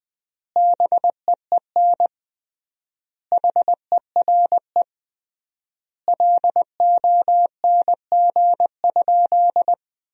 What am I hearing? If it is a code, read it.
Morse code, heard as a keyed tone, about 20 wpm, 700 Hz. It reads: BEEN HERE LONG?
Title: 3x Word-Spacing - Sentences from Top 200 Words - 20wpm